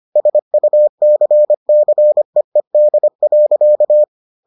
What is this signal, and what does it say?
Morse code, heard as a keyed tone, about 25 wpm, 600 Hz. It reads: SUCCEED.